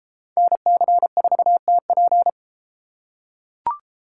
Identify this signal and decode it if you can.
Morse code, heard as a keyed tone, about 30 wpm, 700 Hz. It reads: NC4TP